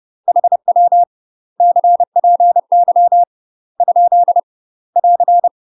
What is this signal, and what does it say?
Morse code, heard as a keyed tone, about 30 wpm, 700 Hz. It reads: HW CPY ? <AR>
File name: UK MASTER QSOs TRAINING / 30WPM / UK MASTER QSOs 0049